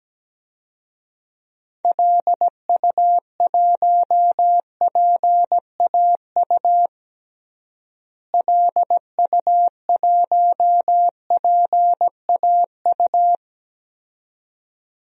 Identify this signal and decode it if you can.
Morse code, heard as a keyed tone, about 17 wpm, 700 Hz. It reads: LU1PAU LU1PAU
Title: DXCC-100 Call Signs - 6 Characters - Repeated 2x - 17wpm